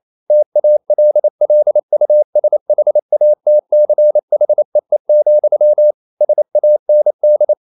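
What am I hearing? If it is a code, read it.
Morse code, heard as a keyed tone, about 28 wpm, 600 Hz. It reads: TALLUSHATCHEE, SAND
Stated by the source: Morse code practice at 28 wpm, text of Ken Follett's novel Never in English, from CW med Gnister.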